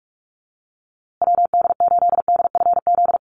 Morse code, heard as a keyed tone, about 48 wpm, 700 Hz. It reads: WD8DFZ
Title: USA Call Signs Repeated 2x 45wpm